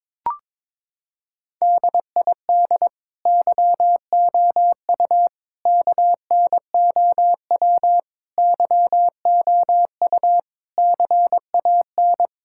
Morse code, heard as a keyed tone, about 22 wpm, 700 Hz. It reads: DID YOU KNOW YOU CAN
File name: Sets of 9 Words - 22wpm